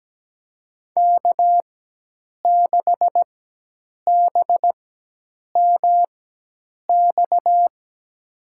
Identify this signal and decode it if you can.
Morse code, heard as a keyed tone, about 17 wpm, spaced as for 8 wpm, 700 Hz. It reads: K6BMX
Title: USA Call Signs - 4x Character Spacing - 17wpm